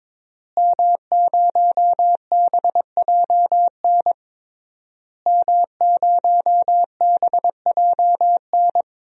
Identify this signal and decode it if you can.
Morse code, heard as a keyed tone, about 22 wpm, 700 Hz. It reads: M0BJN M0BJN
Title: DXCC-100 Call Signs - 5 Characters - Repeated 3x - 22wpm